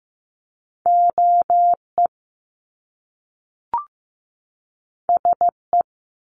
Morse code, heard as a keyed tone, about 15 wpm, 700 Hz. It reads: OE SE
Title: Top Two Letter Combinations in Words 15wpm